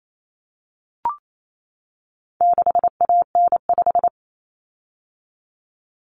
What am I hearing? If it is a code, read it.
Morse code, heard as a keyed tone, about 28 wpm, 700 Hz. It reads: 6AN5
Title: Common Cut Numbers - 4-Digit 28wpm